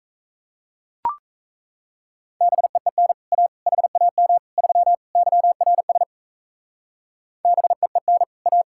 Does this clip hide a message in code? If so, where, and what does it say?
Yes: Morse code from 2.4 to 8.6 s, heard as a keyed tone, about 36 wpm, 700 Hz. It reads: BEEN A HAM 3 YRS BEEN A